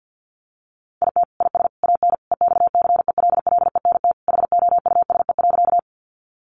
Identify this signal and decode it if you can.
Morse code, heard as a keyed tone, about 50 wpm, 700 Hz. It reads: IT IS AN EXCELLENT HOUSE.
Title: Sentences from Top 1000 Words 50wpm